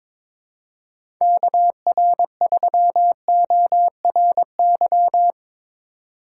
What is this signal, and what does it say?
Morse code, heard as a keyed tone, about 22 wpm, 700 Hz. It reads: KR3ORY
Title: USA Call Signs - 6 Characters - 2x Character Spacing - 22wpm